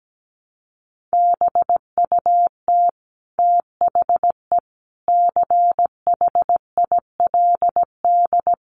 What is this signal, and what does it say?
Morse code, heard as a keyed tone, about 17 wpm, 700 Hz. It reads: BUT THE CHILD
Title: Sentences from Top 1400 Words 17wpm